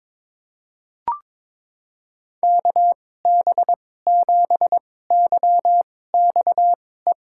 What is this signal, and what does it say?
Morse code, heard as a keyed tone, about 22 wpm, spaced as for 16 wpm, 700 Hz. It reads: KB7YXE
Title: USA Call Signs - 6 Characters - 2x Character Spacing - 22wpm